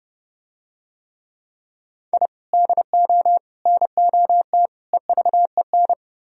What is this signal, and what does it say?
Morse code, heard as a keyed tone, about 31 wpm, 700 Hz. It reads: I DO NOT EVEN